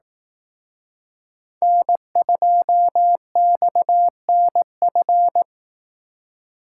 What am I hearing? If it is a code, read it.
Morse code, heard as a keyed tone, about 18 wpm, 700 Hz. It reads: N2XNF
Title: Speed Racing - USA 1x3 Call Signs 15wpm